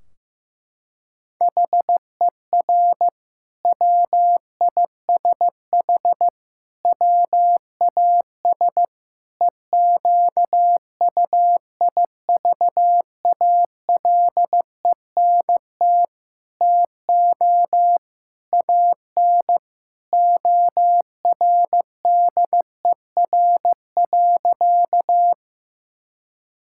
Morse code, heard as a keyed tone, about 15 wpm, 700 Hz. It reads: HER WISH WAS EQUIVALENT TO AN ORDER.
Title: Sentences from Top 1600 Words 15wpm